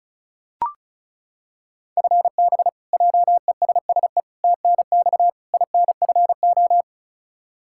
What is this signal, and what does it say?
Morse code, heard as a keyed tone, about 36 wpm, 700 Hz. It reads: FB JESSE TNX INFO
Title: Practice QSO - Exchange 3 - More Info - 35wpm